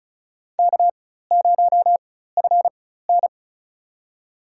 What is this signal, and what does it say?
Morse code, heard as a keyed tone, about 34 wpm, spaced as for 16 wpm, 700 Hz. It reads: K0FN
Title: USA Call Signs - 4 Characters - 4x Character Spacing - 35wpm